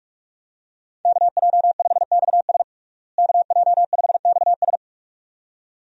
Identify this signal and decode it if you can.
Morse code, heard as a keyed tone, about 42 wpm, 700 Hz. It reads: KJ5XS KJ5XS
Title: USA Call Signs - 5 Characters - Repeated 2x - 45wpm